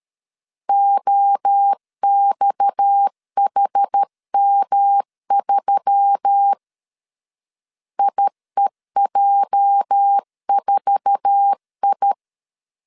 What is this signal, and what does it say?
Morse code, heard as a keyed tone, about 13 wpm, 800 Hz. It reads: OXHM3 IEJ4I
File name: CW / DET / 1 9 1500C S A 12wpm 3800